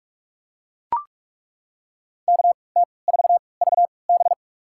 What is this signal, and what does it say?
Morse code, heard as a keyed tone, about 42 wpm, spaced as for 25 wpm, 700 Hz. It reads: KT4VB
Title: USA Call Signs - 5 Characters - 3x Character Spacing - 45wpm